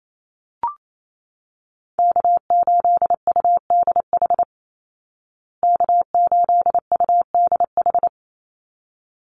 Morse code, heard as a keyed tone, about 28 wpm, 700 Hz. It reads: K8UDH K8UDH
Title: Call Signs Repeated 3x - 28wpm